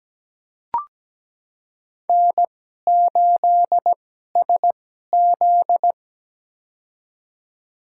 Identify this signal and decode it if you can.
Morse code, heard as a keyed tone, about 17 wpm, spaced as for 12 wpm, 700 Hz. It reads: N8SZ
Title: USA Call Signs - 4 Characters - 2x Character Spacing - 17wpm